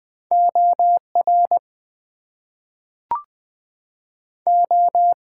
Morse code, heard as a keyed tone, about 20 wpm, 700 Hz. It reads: OR O
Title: Two Letter Words - 20wpm